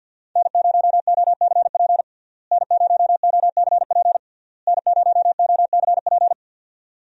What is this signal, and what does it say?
Morse code, heard as a keyed tone, about 52 wpm, 700 Hz. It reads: N0OCP N0OCP N0OCP
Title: USA Call Signs - 5 Characters - Repeated 3x - 50wpm